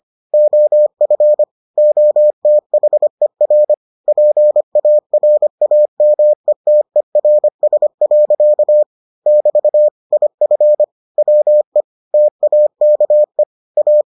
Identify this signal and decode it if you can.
Morse code, heard as a keyed tone, about 25 wpm, 600 Hz. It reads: OF OTHER PARAMETERS. <BT> IF WE TAKE A